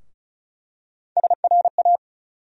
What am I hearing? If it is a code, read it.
Morse code, heard as a keyed tone, about 36 wpm, 700 Hz. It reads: SRA
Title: USA Call Sign Postfixes - 3 Letters 35wpm